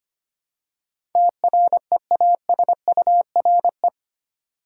Morse code, heard as a keyed tone, about 25 wpm, 700 Hz. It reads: TREASURE
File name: Eight Letter Words - 25wpm